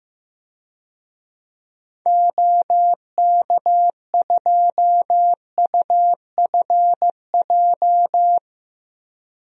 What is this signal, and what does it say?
Morse code, heard as a keyed tone, about 15 wpm, 700 Hz. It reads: OK2UFJ
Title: DXCC-100 Call Signs - 6 Characters - 15wpm